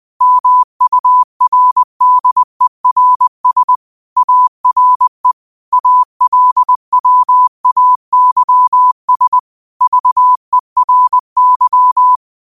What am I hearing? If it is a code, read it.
Morse code, heard as a keyed tone, about 20 wpm, 1000 Hz. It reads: MURDERS ARE ALWAYS VERY